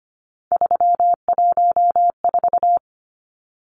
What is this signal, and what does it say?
Morse code, heard as a keyed tone, about 25 wpm, 700 Hz. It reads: 314
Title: Numbers - 3-Digit 25wpm